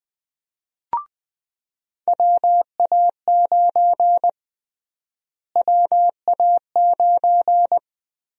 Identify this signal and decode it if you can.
Morse code, heard as a keyed tone, about 20 wpm, 700 Hz. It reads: WA9 WA9
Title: USA Call Sign Prefixes - Repeated 2x - 20wpm